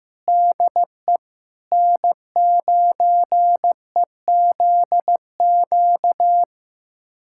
Morse code, heard as a keyed tone, about 15 wpm, 700 Hz. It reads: DE N9EZQ
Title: USA Callsign DE Callsign - 5 Characters - 15wpm